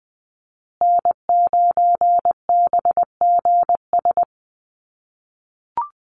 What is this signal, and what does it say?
Morse code, heard as a keyed tone, about 20 wpm, 700 Hz. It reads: N9BGS
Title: USA Call Signs Repeated 3x 20wpm